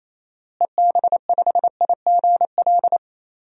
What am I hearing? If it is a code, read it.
Morse code, heard as a keyed tone, about 28 wpm, 700 Hz. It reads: EB5IGL
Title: DXCC-100 Call Signs - 6 Characters - Repeated 3x - 28wpm